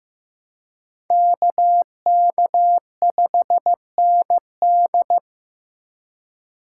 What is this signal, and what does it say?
Morse code, heard as a keyed tone, about 15 wpm, 700 Hz. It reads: KK5ND
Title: USA Call Signs - 5 Characters - 3x Character Spacing - 15wpm